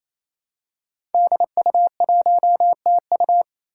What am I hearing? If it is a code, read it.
Morse code, heard as a keyed tone, about 28 wpm, 700 Hz. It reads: DU1TU
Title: Five Challenging Letter-Number Combinations 28wpm